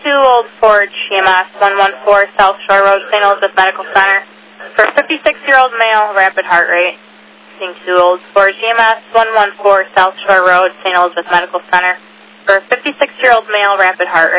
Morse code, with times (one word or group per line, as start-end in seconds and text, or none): none